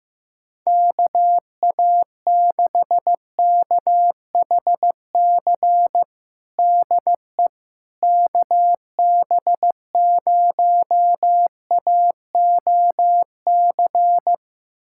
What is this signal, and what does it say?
Morse code, heard as a keyed tone, about 16 wpm, 700 Hz. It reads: KA6KHC DE KB0AOC
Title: USA Callsign DE Callsign - 6 Characters - 15wpm